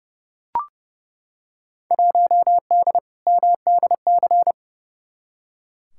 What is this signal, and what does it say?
Morse code, heard as a keyed tone, about 30 wpm, 700 Hz. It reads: 1D MDC
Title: Field Day Exchanges - 30wpm